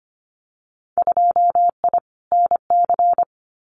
Morse code, heard as a keyed tone, about 25 wpm, 700 Hz. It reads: 2I NC
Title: Winter Field Day Exchanges 25wpm